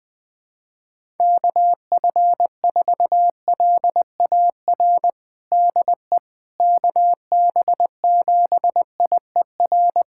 Morse code, heard as a keyed tone, about 20 wpm, 700 Hz. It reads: KF4LAR DE KB7IER
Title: USA Callsign DE Callsign - 6 Characters - 20wpm